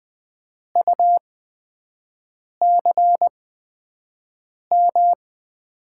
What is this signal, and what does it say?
Morse code, heard as a keyed tone, about 20 wpm, spaced as for 5 wpm, 700 Hz. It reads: UCM
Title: USA Call Signs - 5 Characters - 8x Character Spacing - 20wpm